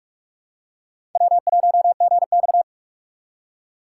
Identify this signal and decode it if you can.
Morse code, heard as a keyed tone, about 46 wpm, 700 Hz. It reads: W1GX
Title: USA Call Signs - 4 Characters - 4x Character Spacing - 45wpm